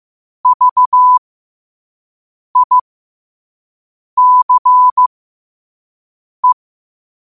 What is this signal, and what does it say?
Morse code, heard as a keyed tone, about 15 wpm, spaced as for 5 wpm, 1000 Hz. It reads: VICE
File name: qotdfiles / 5wpm